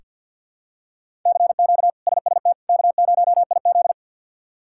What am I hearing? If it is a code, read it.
Morse code, heard as a keyed tone, about 46 wpm, 700 Hz. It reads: CQ SST K0IZ
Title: K1USN SST QSO Practice - 50wpm